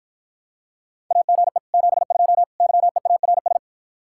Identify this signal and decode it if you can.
Morse code, heard as a keyed tone, about 52 wpm, 700 Hz. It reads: AGE 72 YEARS